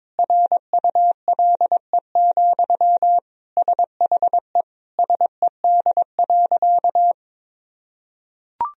Morse code, heard as a keyed tone, about 22 wpm, 700 Hz. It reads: RULE, SHE SED.
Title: Sentences from Top 400 Words and Abbreviations 22wpm